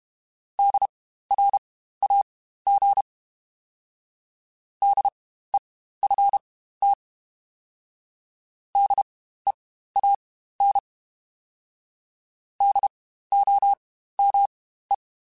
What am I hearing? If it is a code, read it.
Morse code, heard as a keyed tone, about 32 wpm, spaced as for 15 wpm, 800 Hz. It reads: DRAG DEFT DEAN DOME